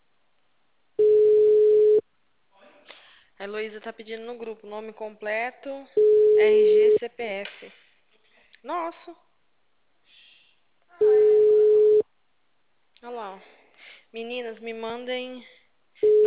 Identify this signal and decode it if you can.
Ringback tone (Europe, 425 Hz), with speech.